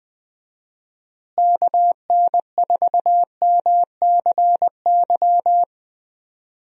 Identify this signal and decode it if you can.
Morse code, heard as a keyed tone, about 20 wpm, 700 Hz. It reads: KN4MCY
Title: USA Call Signs - 6 Characters - Repeated 2x - 20wpm